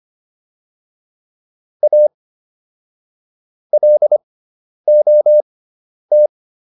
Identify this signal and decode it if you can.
Morse code, heard as a keyed tone, about 25 wpm, spaced as for 10 wpm, 600 Hz. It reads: A LOT